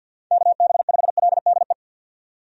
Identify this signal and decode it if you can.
Morse code, heard as a keyed tone, about 52 wpm, 700 Hz. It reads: KB5LDE
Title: USA Call Signs - 6 Characters - 50wpm